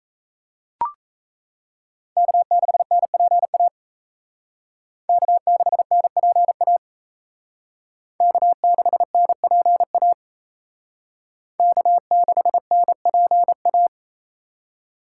Morse code, heard as a keyed tone, about 32 wpm, 700 Hz. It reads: K6NPA K6NPA K6NPA K6NPA
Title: Speed Racing - USA 1x3 Call Signs 28wpm